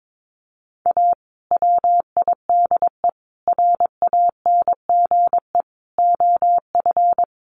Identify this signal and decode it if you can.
Morse code, heard as a keyed tone, about 22 wpm, 700 Hz. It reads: A WIDE RANGE OF
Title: Sets of 4 Words 22wpm